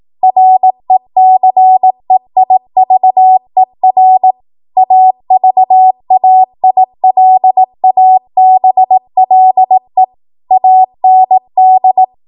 Morse code, heard as a keyed tone, about 18 wpm, 750 Hz. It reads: RECEIVER AVAILABLE AND
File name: Morse / 18WPM